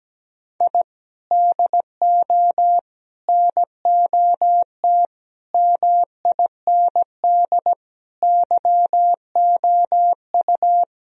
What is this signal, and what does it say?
Morse code, heard as a keyed tone, about 17 wpm, 700 Hz. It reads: I DO NOT MIND YOU
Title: Sentences from Top 1000 Words - Review Part 3 - 17wpm